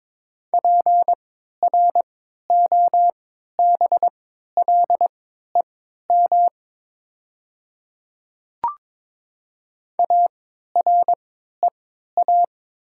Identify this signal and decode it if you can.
Morse code, heard as a keyed tone, about 22 wpm, spaced as for 12 wpm, 700 Hz. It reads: PROBLEM AREA
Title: 3x Character Spacing - Top 500 Words - 22wpm